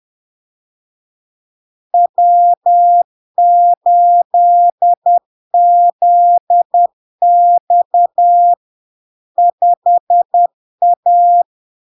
Morse code, heard as a keyed tone, about 10 wpm, 700 Hz. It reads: W8ZX 5A
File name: LARK-FD-2022-Log-in-CW-10wpm-700-Hz